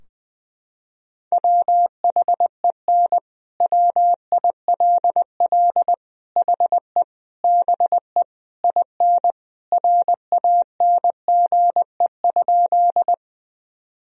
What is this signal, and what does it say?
Morse code, heard as a keyed tone, about 21 wpm, 700 Hz. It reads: WHEN WILL HE BE IN RANGE?
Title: Sentences from Top 1000 Words - Review Part 4 - 20wpm